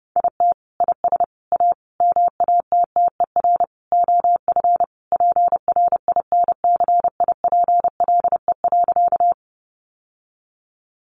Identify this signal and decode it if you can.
Morse code, heard as a keyed tone, about 30 wpm, 700 Hz. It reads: IT IS A MATTER OF PRINCIPLE.
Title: Sentences from Top 1400 Words 30wpm